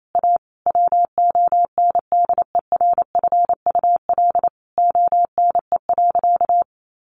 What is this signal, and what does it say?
Morse code, heard as a keyed tone, about 28 wpm, 700 Hz. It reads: A WONDERFUL ONE.
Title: Sentences from Top 1100 Words 28wpm